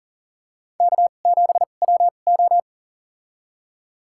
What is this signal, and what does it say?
Morse code, heard as a keyed tone, about 39 wpm, spaced as for 29 wpm, 690 Hz. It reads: K7WO
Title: USA Call Signs - 4 Characters - 2x Character Spacing - 40wpm